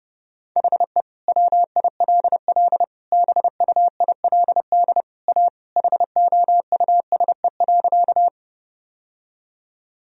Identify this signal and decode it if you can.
Morse code, heard as a keyed tone, about 32 wpm, 700 Hz. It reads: HE WILL BUILD A HOUSE.
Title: Sentences from Top 1000 Words - Review Part 1 - 30wpm